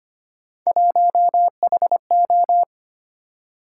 Morse code, heard as a keyed tone, about 25 wpm, 700 Hz. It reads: 1HO
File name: Three Letter-Number Combinations - 25wpm